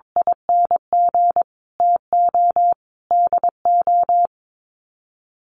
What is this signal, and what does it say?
Morse code, heard as a keyed tone, about 22 wpm, 700 Hz. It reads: ING TO DO
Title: Sets of 5 Words Encore2 22wpm